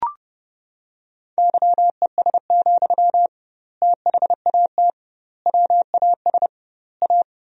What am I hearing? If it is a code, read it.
Morse code, heard as a keyed tone, about 32 wpm, 700 Hz. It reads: YES, THAT WAS A
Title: 2x Word-Spacing - Sentences from Top 500 Words - 30wpm